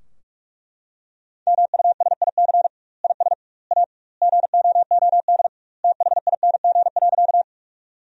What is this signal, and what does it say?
Morse code, heard as a keyed tone, about 41 wpm, 700 Hz. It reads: MUSIC IS A GOOD THING.
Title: 2x Word-Spacing - Sentences from Top 300 Words - 45wpm